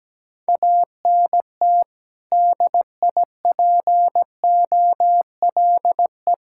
Morse code, heard as a keyed tone, about 17 wpm, 700 Hz. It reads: ANT DIPOLE
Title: Practice QSO - Exchange 2 - Info - 17wpm